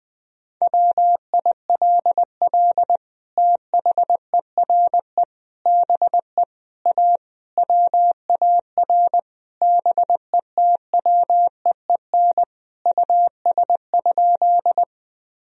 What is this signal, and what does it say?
Morse code, heard as a keyed tone, about 20 wpm, 700 Hz. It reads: WILL THERE BE A WAR BETWEEN US?
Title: Sentences from Top 1000 Words - Review Part 2 - 20wpm